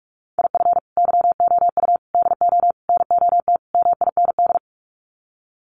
Morse code, heard as a keyed tone, about 44 wpm, 700 Hz. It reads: IF YOU DO NOT MIND
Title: Sets of 5 Words 45wpm